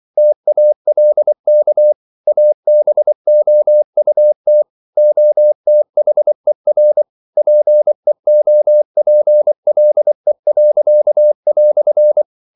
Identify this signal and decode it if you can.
Morse code, heard as a keyed tone, about 24 wpm, 600 Hz. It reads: TALK ABOUT OTHER PEOPLE."
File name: CW / Practice / EnglishQuotations0324